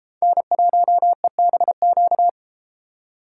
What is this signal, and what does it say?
Morse code, heard as a keyed tone, about 31 wpm, 700 Hz. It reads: N1EBQ